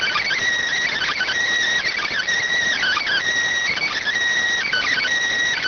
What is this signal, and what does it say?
Emergency Alert System (SAME) tones (header bursts), heard as harsh buzzing bursts. It is heard from 0.8 to 5.1 s.